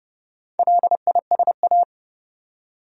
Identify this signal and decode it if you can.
Morse code, heard as a keyed tone, about 32 wpm, 700 Hz. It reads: LISA
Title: USA Common Names - Max 4 Letters - 30wpm